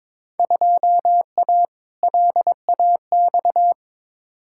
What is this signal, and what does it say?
Morse code, heard as a keyed tone, about 22 wpm, 700 Hz. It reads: 2A LAX
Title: Field Day Exchanges - 22wpm